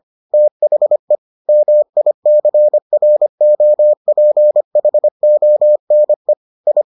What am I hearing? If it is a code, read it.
Morse code, heard as a keyed tone, about 25 wpm, 600 Hz. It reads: THE MICROPHONE I